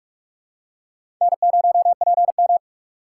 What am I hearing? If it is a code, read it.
Morse code, heard as a keyed tone, about 44 wpm, 700 Hz. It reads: N0PM
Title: USA Call Signs - 4 Characters - 6x Character Spacing - 45wpm